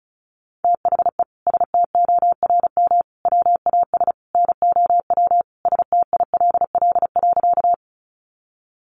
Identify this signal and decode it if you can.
Morse code, heard as a keyed tone, about 34 wpm, 700 Hz. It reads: THE STORM WAS NOW STILL.
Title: Sentences from Top 1500 Words 35wpm